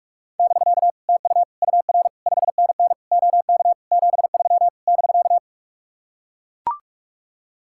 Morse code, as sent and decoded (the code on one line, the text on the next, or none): -...-.- / - ..- / ..- .-. / ..... -. -. / --- -.- / --... ...-- / -...-.-
<BK> TU UR 5NN OK 73 <BK>